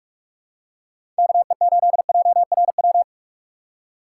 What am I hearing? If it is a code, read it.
Morse code, heard as a keyed tone, about 45 wpm, 700 Hz. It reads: KE8JRW
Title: USA Call Signs - 2x Character Spacing - 45wpm